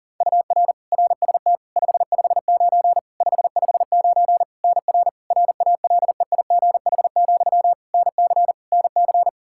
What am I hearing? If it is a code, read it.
Morse code, heard as a keyed tone, about 40 wpm, 700 Hz. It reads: UR RST 559 559 NR RALEIGH, NC NC
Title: K1USN SST QSO Practice 40wpm